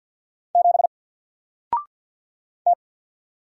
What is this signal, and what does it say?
Morse code, heard as a keyed tone, about 48 wpm, 700 Hz. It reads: 7 T
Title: Single Character (Letter:Number:Punctuation:Prosign) Alt 50wpm